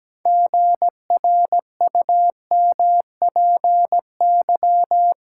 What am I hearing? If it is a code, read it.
Morse code, heard as a keyed tone, about 17 wpm, 700 Hz. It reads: GRUMPY
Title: Pangram Challenge v2 - 17wpm